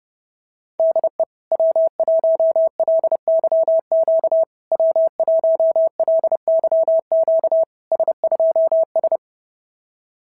Morse code, heard as a keyed tone, about 30 wpm, 650 Hz. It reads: DE W1LYQ W1LYQ S2S
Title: SOTA QSO Practice 30wpm